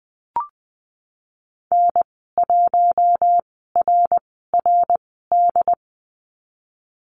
Morse code, heard as a keyed tone, about 20 wpm, spaced as for 15 wpm, 700 Hz. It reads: N1RRD
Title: 076 Lesson 13g - Letter L - Call Signs 20wpm